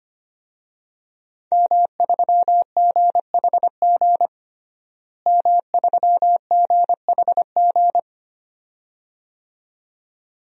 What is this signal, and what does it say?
Morse code, heard as a keyed tone, about 25 wpm, 700 Hz. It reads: M3GHG M3GHG
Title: DXCC-100 Call Signs - 5 Characters - Repeated 2x - 25wpm